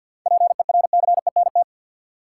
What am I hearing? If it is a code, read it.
Morse code, heard as a keyed tone, about 53 wpm, 700 Hz. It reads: PERCENT
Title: Sets of 2 Words - New Word Drill - 50wpm